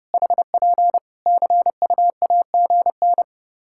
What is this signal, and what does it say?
Morse code, heard as a keyed tone, about 30 wpm, 700 Hz. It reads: HP CUAGN
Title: Practice QSO - Exchange 4 - Ending - 30wpm